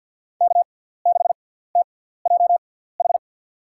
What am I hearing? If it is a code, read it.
Morse code, heard as a keyed tone, about 50 wpm, spaced as for 17 wpm, 700 Hz. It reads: K6TJH
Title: USA Call Signs - 6x Character Spacing - 50wpm